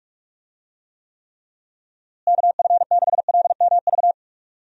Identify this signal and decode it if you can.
Morse code, heard as a keyed tone, about 41 wpm, 700 Hz. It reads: KF6LMV
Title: USA Call Signs - 6 Characters - Repeated 3x - 45wpm